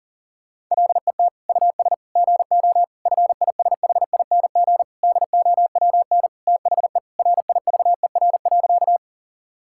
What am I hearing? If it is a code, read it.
Morse code, heard as a keyed tone, about 40 wpm, 700 Hz. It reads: LET US GO FISHING DOWN THE RIVER.
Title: Sentences from Top 1000 Words - Review Part 5 - 40wpm